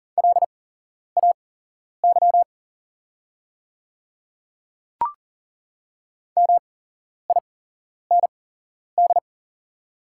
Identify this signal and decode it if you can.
Morse code, heard as a keyed tone, about 39 wpm, spaced as for 11 wpm, 690 Hz. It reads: LAY MIND
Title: 8x Character Spacing - Top 500 Words - 40wpm